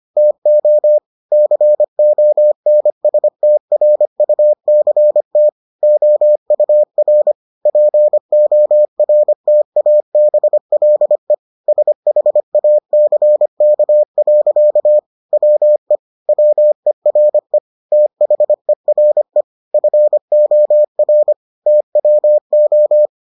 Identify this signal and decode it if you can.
Morse code, heard as a keyed tone, about 25 wpm, 600 Hz. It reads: TO CONSTRUCT OUR PORTABLE SHACK. WE WERE THERE FOR TWO